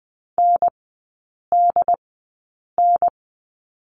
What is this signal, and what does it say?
Morse code, heard as a keyed tone, about 20 wpm, spaced as for 8 wpm, 700 Hz. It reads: NDN
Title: 100 Lesson 17e - Letter u - Three Characters 20wpm